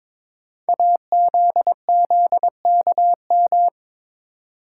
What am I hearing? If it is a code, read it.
Morse code, heard as a keyed tone, about 22 wpm, 700 Hz. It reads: AZZKM
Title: Code Groups - 5 Letters - 22wpm